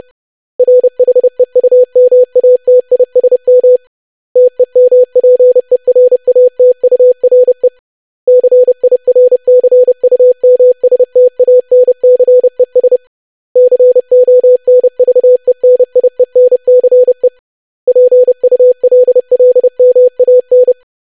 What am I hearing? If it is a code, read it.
Morse code, heard as a keyed tone, about 30 wpm, 500 Hz. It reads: RHEUMATISM TEMPERATURE CIRCUMSTANCES CONVENIENCE PULLMAN
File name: lesson / web / longwords-25wpm